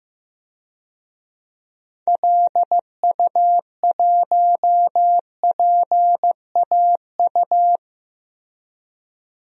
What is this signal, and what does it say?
Morse code, heard as a keyed tone, about 15 wpm, 700 Hz. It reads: LU1PAU
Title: DXCC-100 Call Signs - 6 Characters - 15wpm